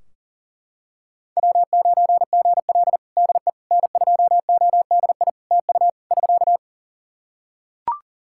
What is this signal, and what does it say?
Morse code, heard as a keyed tone, about 42 wpm, 700 Hz. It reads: W9GL DE N2ODI TU <SK>
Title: Practice QSO - Exchange 4 - Ending - 40wpm